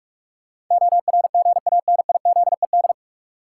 Morse code, heard as a keyed tone, about 46 wpm, 700 Hz. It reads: ORGANIZED
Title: Sentences from Top 1300 Words - New Word Drill Alt - 45wpm